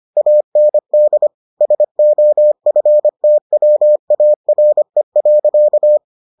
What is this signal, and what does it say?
Morse code, heard as a keyed tone, about 25 wpm, 600 Hz. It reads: AND SOFTWARE.